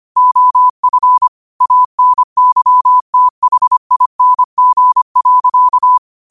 Morse code, heard as a keyed tone, about 25 wpm, 1000 Hz. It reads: OF ANYTHING.
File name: qotdfiles / 25wpm